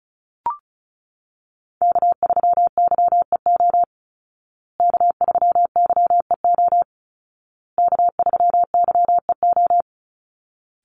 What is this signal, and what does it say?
Morse code, heard as a keyed tone, about 34 wpm, 700 Hz. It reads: K3YEO K3YEO K3YEO
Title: Call Signs Repeated 2x - 35wpm